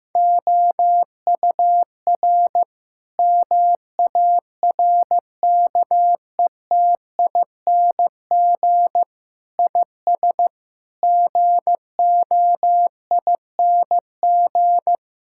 Morse code, heard as a keyed tone, about 15 wpm, 700 Hz. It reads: OUR MARKETING IS GOING
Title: Sentences from Top 1000 Words - Review Part 5 - 15wpm